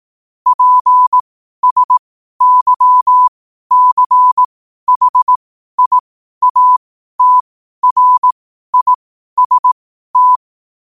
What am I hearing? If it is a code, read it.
Morse code, heard as a keyed tone, about 18 wpm, spaced as for 12 wpm, 1000 Hz. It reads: PSYCHIATRIST
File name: qotdfiles / 12wpm